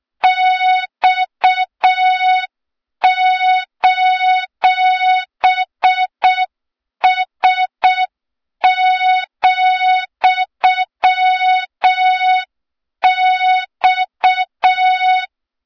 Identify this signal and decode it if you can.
Morse code, heard as a keyed tone, about 6 wpm, 3700 Hz. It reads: X:S,X
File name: Morze-HA2MN / hang / 23 GYAKORLAT